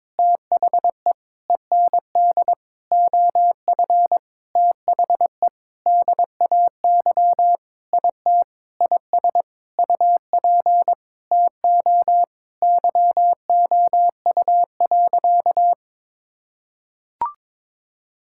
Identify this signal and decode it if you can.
Morse code, heard as a keyed tone, about 22 wpm, 700 Hz. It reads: THE END OF THE DAY IT IS UP TO YOU.
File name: Sentences with 7-Word Phrases - 22wpm